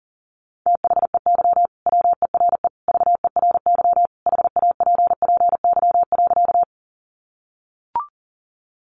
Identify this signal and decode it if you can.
Morse code, heard as a keyed tone, about 40 wpm, 700 Hz. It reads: THEY WERE VERY HAPPY.
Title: Sentences from Top 1000 Words 40wpm